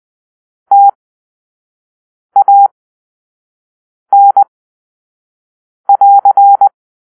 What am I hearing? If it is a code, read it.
Morse code, heard as a keyed tone, about 20 wpm, spaced as for 5 wpm, 800 Hz. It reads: TAN<AR>
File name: arrl / Grp-1 Drill-2